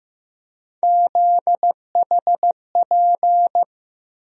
Morse code, heard as a keyed tone, about 16 wpm, 700 Hz. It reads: ZHP
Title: USA Call Sign Postfixes - 3 Letters 15wpm